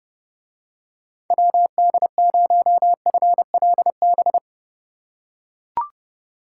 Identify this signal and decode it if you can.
Morse code, heard as a keyed tone, about 31 wpm, 700 Hz. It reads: WD0FLB